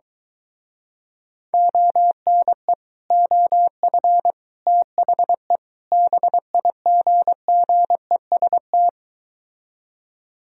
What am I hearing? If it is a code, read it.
Morse code, heard as a keyed tone, about 23 wpm, 700 Hz. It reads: ONE OF THE BIGGEST